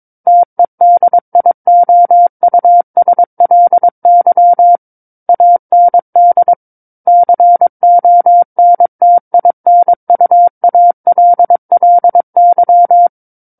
Morse code, heard as a keyed tone, about 22 wpm, 700 Hz. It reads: TEDIOUSLY AND CONTINUALLY
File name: Book One / 13-Chapter 13 - 22 WPM